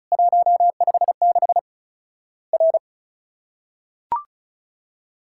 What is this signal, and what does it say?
Morse code, heard as a keyed tone, about 36 wpm, 700 Hz. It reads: 156 R
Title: Worked All Europe DX Contest - QTC Exchange - 35wpm